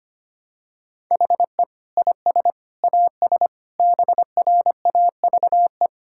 Morse code, heard as a keyed tone, about 25 wpm, 700 Hz. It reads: HE IS AS BRAVE
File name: practice / 25wpm / Difficult Sentences with Hints - 25wpm